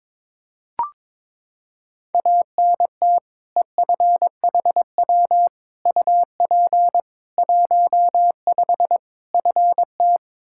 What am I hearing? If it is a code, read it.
Morse code, heard as a keyed tone, about 22 wpm, 700 Hz. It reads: ANT EFHW UP 15 FT
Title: Practice QSO - Exchange 2 - Info - 22wpm